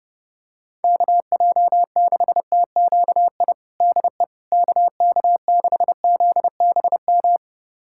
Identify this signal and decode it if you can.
Morse code, heard as a keyed tone, about 32 wpm, 700 Hz. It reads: KJ6TQI DE KK6ZBM
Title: USA Callsign DE Callsign 30wpm